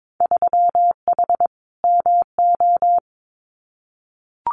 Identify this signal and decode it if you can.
Morse code, heard as a keyed tone, about 22 wpm, 700 Hz. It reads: 3H MO
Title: Winter Field Day Exchanges Encore 22wpm